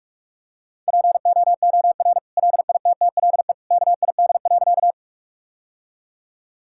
Morse code, heard as a keyed tone, about 41 wpm, 700 Hz. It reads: POOR LITTLE KID.
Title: Sentences from Top 1600 Words 45wpm